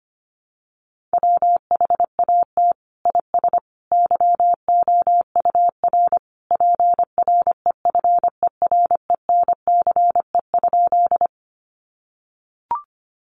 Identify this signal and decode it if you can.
Morse code, heard as a keyed tone, about 25 wpm, 700 Hz. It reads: WHAT IS YOUR PREFERENCE?
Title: Sentences from Top 1500 Words 25wpm